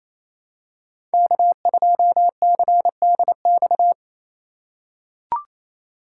Morse code, heard as a keyed tone, about 27 wpm, 700 Hz. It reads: K2CDX
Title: K1USN SST QSO Practice - 28wpm